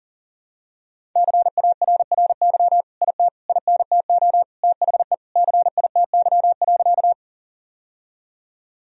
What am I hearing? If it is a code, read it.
Morse code, heard as a keyed tone, about 37 wpm, 700 Hz. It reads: CARRY IT INTO THE CITY.
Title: Sentences from Top 1000 Words - Review Part 1 - 40wpm